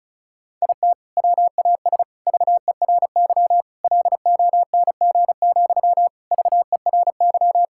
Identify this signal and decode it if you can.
Morse code, heard as a keyed tone, about 34 wpm, 700 Hz. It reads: IT WAS VERY LONG, VERY
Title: Sentences from Top 1600 Words 35wpm